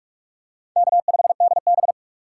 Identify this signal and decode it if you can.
Morse code, heard as a keyed tone, about 42 wpm, 700 Hz. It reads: K5DB
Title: USA Call Signs - 4 Characters - 3x Character Spacing - 45wpm